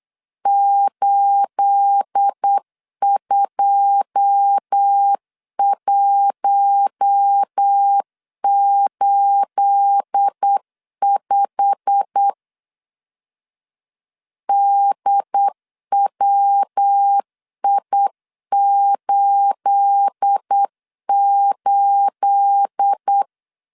Morse code, heard as a keyed tone, about 8 wpm, 800 Hz. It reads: 82185 DWI88
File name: LEZIONI / 9 28 8wpm 3512